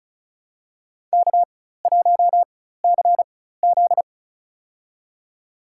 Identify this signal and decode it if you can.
Morse code, heard as a keyed tone, about 35 wpm, spaced as for 16 wpm, 700 Hz. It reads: K1CZ
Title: USA Call Signs - 4 Characters - 4x Character Spacing - 35wpm